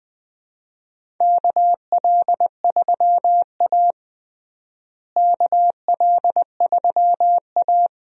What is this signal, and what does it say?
Morse code, heard as a keyed tone, about 20 wpm, 700 Hz. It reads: KL3A KL3A
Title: USA Call Signs - 4 Characters - Repeated 2x - 20wpm